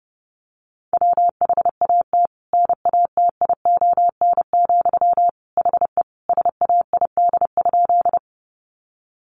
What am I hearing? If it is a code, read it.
Morse code, heard as a keyed tone, about 30 wpm, 700 Hz. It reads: WHAT NATION, HE SAID?
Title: Sentences from Top 900 Words 30wpm